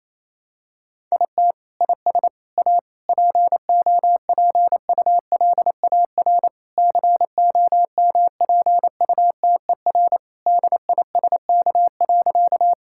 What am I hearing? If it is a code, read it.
Morse code, heard as a keyed tone, about 27 wpm, 700 Hz. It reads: IT IS A POPULAR COMPUTER DISK.
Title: Sentences from Top 1000 Words - Review Part 6 - 28wpm